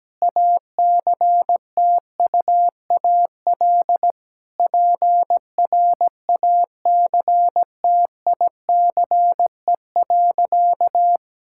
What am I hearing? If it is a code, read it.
Morse code, heard as a keyed tone, about 17 wpm, 700 Hz. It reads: ACTUAL PRACTICE.
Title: Sentences from Top 1000 Words - Review Part 5 - 17wpm